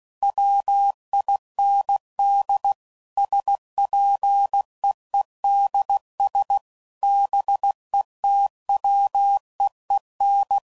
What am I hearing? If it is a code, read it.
Morse code, heard as a keyed tone, about 16 wpm, 800 Hz. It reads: WIND SPEEDS BETWEEN
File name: CW Code Groups @15WPM / Weather Sets / Weather 4 24